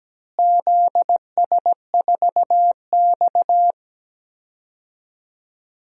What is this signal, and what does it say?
Morse code, heard as a keyed tone, about 17 wpm, 700 Hz. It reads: ZS4X
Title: DXCC-100 Call Signs - 4 Characters - Repeated 3x - 17wpm